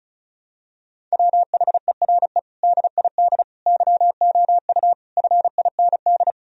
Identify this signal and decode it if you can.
Morse code, heard as a keyed tone, about 34 wpm, 700 Hz. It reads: WHERE DID YOU FIND
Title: Sentences from Top 1000 Words - Review Part 2 - 35wpm